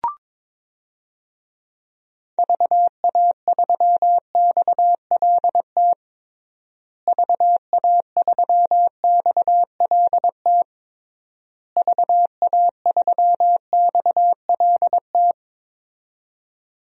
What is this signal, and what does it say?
Morse code, heard as a keyed tone, about 22 wpm, 700 Hz. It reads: VA3XLT VA3XLT VA3XLT
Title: DXCC-100 Call Signs - 6 Characters - Repeated 3x - 22wpm